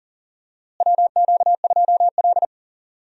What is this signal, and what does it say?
Morse code, heard as a keyed tone, about 40 wpm, 700 Hz. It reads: WQ2L
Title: USA Call Signs - 4 Characters - Repeated 2x - 40wpm